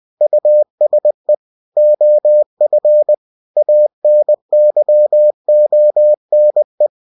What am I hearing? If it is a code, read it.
Morse code, heard as a keyed tone, about 20 wpm, 600 Hz. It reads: USE OF ANYONE
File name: winnie-20wpm